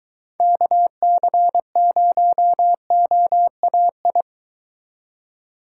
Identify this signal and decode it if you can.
Morse code, heard as a keyed tone, about 23 wpm, 700 Hz. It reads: KC0OAI